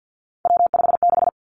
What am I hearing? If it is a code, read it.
Morse code, heard as a keyed tone, about 50 wpm, 700 Hz. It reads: W56
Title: Three Challenging Letter-Number Combinations 50wpm